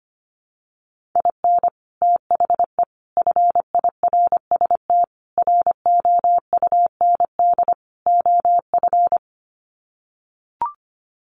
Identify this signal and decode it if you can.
Morse code, heard as a keyed tone, about 25 wpm, 700 Hz. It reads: IN THE FIRST ROUND OF
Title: Sets of 5 Words Encore 25wpm